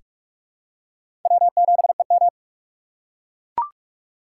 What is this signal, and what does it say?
Morse code, heard as a keyed tone, about 44 wpm, 700 Hz. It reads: W7EM
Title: USA Call Signs - 4 Characters - 3x Character Spacing - 45wpm